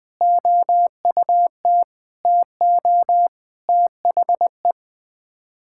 Morse code, heard as a keyed tone, about 20 wpm, 700 Hz. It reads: OUT TO THE
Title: Sets of 8 Words - 20wpm